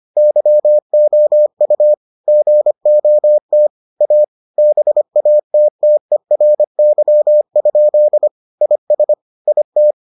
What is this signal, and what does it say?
Morse code, heard as a keyed tone, about 25 wpm, 600 Hz. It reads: YOU GOT A BATTERY? IS IT